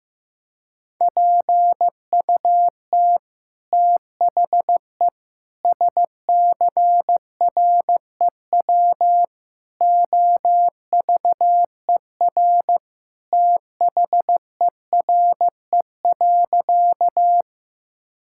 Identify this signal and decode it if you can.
Morse code, heard as a keyed tone, about 16 wpm, 700 Hz. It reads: PUT THE SCREW OVER THERE.
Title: Sentences from Top 1600 Words 15wpm